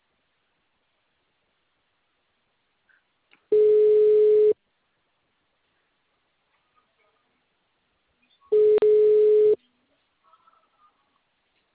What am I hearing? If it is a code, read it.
Ringback tone (Europe, 425 Hz).
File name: out- -2011- - - .74847